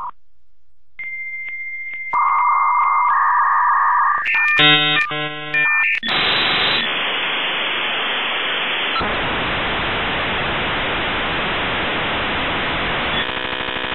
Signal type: dial-up modem handshake